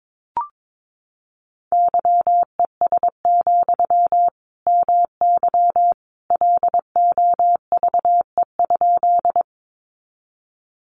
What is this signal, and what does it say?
Morse code, heard as a keyed tone, about 22 wpm, 700 Hz. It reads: YES, MY LOVE?
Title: Sentences from Top 500 Words 22wpm